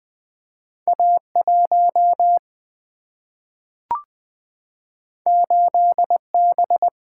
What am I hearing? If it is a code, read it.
Morse code, heard as a keyed tone, about 20 wpm, 700 Hz. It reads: A1 8B
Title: Single-Letter Single-Number 20wpm